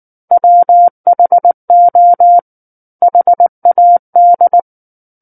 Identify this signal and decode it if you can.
Morse code, heard as a keyed tone, about 19 wpm, 700 Hz. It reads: WHO HAD
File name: Book One / 10-Chapter 10 - 19 WPM